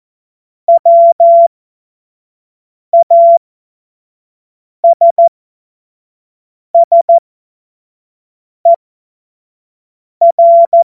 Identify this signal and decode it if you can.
Morse code, heard as a keyed tone, about 14 wpm, spaced as for 5 wpm, 680 Hz. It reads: WASSER
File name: Morsekurs / Witze / Witz 21. Dezember 2021, 25 F70